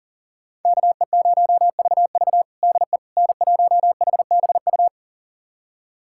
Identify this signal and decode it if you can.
Morse code, heard as a keyed tone, about 37 wpm, 700 Hz. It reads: KE0VV DE N1HBU
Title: USA Callsign DE Callsign - 5 Characters - 40wpm